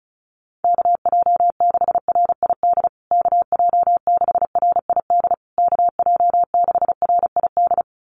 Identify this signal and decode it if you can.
Morse code, heard as a keyed tone, about 34 wpm, 700 Hz. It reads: KJ6RID KJ6RID KJ6RID
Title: USA Call Signs Repeated 3x 35wpm